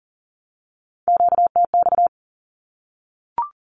Morse code, heard as a keyed tone, about 38 wpm, 700 Hz. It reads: QTX
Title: Q-Codes 40wpm